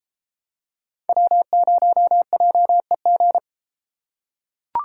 Morse code, heard as a keyed tone, about 33 wpm, 700 Hz. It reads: W0JEG